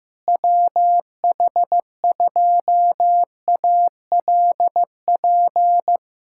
Morse code, heard as a keyed tone, about 15 wpm, 700 Hz. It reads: WH2ALP
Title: USA Callsign DE Callsign - 6 Characters - 15wpm